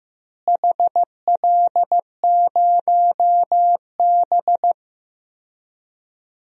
Morse code, heard as a keyed tone, about 15 wpm, 700 Hz. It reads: HL0B
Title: DXCC-100 Call Signs - 4 Characters - Repeated 2x - 15wpm